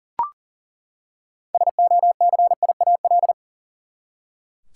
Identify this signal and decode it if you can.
Morse code, heard as a keyed tone, about 38 wpm, 700 Hz. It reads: SOCIAL
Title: Practice QSO - Exchange 3 - More Info - 40wpm